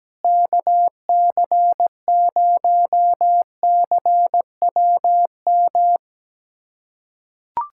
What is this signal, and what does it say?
Morse code, heard as a keyed tone, about 17 wpm, 700 Hz. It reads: KC0CWM
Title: USA Callsign DE Callsign 17wpm